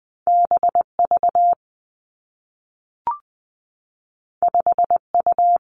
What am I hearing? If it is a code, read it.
Morse code, heard as a keyed tone, about 20 wpm, 700 Hz. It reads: B V 5 U
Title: Two Challenging Letter-Number Combinations 20wpm